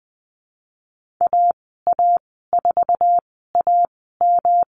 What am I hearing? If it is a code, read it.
Morse code, heard as a keyed tone, about 20 wpm, spaced as for 14 wpm, 700 Hz. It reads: AA4AM
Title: 122 Lesson 20g - Number 3 - Call Signs 20wpm